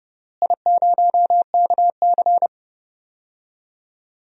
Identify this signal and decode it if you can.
Morse code, heard as a keyed tone, about 32 wpm, 700 Hz. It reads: I0KC
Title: DXCC-100 Call Signs - 4 Characters - Repeated 2x - 30wpm